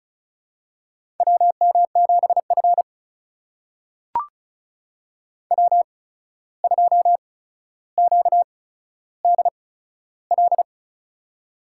Morse code, heard as a keyed tone, about 36 wpm, spaced as for 14 wpm, 700 Hz. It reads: WM7F W2QDL
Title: USA Call Signs - 8x Character Spacing - 35wpm